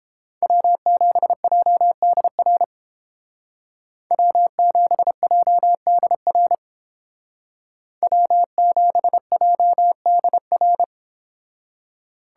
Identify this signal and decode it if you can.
Morse code, heard as a keyed tone, about 29 wpm, 700 Hz. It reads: W7JDR W7JDR W7JDR